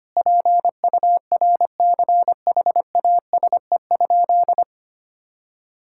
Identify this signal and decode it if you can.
Morse code, heard as a keyed tone, about 25 wpm, 700 Hz. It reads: PURCHASE?
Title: Sentences from Top 1000 Words - Review Part 5 - 25wpm